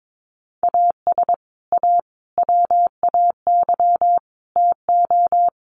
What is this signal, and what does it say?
Morse code, heard as a keyed tone, about 22 wpm, 700 Hz. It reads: AS A WAY TO
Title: Sets of 4 Words 22wpm